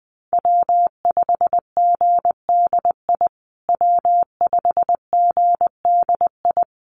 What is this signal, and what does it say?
Morse code, heard as a keyed tone, about 20 wpm, 700 Hz. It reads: W5GDI W5GDI
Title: USA Call Signs Repeated 2x 20wpm